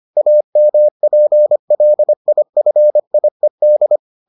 Morse code, heard as a keyed tone, about 25 wpm, 600 Hz. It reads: AMPLIFIED